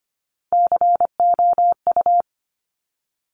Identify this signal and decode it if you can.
Morse code, heard as a keyed tone, about 25 wpm, 700 Hz. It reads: COU
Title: Top Three Letter Combinations in Words 25wpm